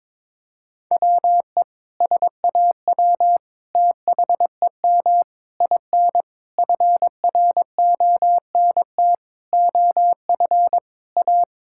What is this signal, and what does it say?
Morse code, heard as a keyed tone, about 22 wpm, 700 Hz. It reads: WE SAW THEM IN FRONT OF A
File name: practice / 22wpm / Sentences with 4-Word Phrases - 22wpm